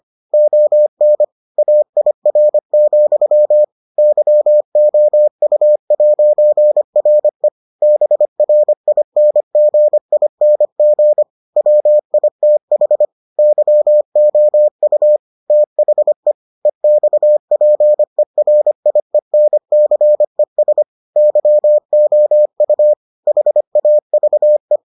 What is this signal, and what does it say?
Morse code, heard as a keyed tone, about 25 wpm, 600 Hz. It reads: ON AIR, YOU'RE BRINGING WITH YOU THE EXPERIENCES YOU HAVE